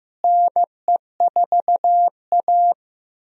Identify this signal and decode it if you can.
Morse code, heard as a keyed tone, about 16 wpm, 700 Hz. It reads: NE4A
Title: USA Call Signs - 4 Characters - Repeated 3x - 15wpm